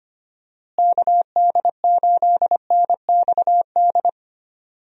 Morse code, heard as a keyed tone, about 25 wpm, 700 Hz. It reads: KD8NXD
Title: USA Call Signs - 6 Characters - Repeated 3x - 25wpm